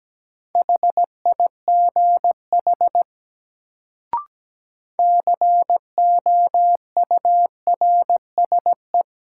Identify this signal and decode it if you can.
Morse code, heard as a keyed tone, about 17 wpm, 700 Hz. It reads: HIGH COURSE
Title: Sets of 2 Words - New Word Drill - 17wpm